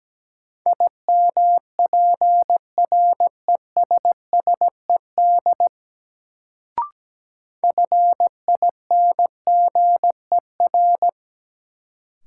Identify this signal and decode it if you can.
Morse code, heard as a keyed tone, about 17 wpm, 700 Hz. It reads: IMPRESSED FINGER
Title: Sentences from Top 1600 Words - New Word Drill Alt - 17wpm